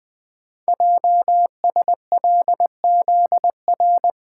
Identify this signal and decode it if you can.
Morse code, heard as a keyed tone, about 20 wpm, 700 Hz. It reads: JSLZR
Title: Code Groups - 5 Letters - 20wpm